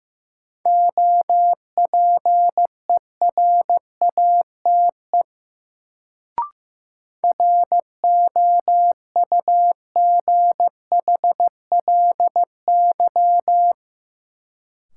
Morse code, heard as a keyed tone, about 16 wpm, 700 Hz. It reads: OPERATE ROUGHLY
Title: Sentences from Top 1500 Words - New Word Drill Alt - 15wpm